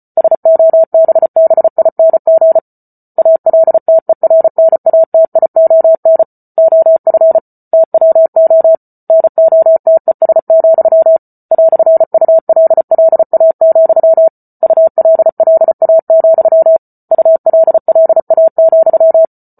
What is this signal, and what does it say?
Morse code, heard as a keyed tone, about 34 wpm, 650 Hz. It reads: SOBBING ALTERNATION OF TWO NOTES, "ULLA, ULLA, ULLA,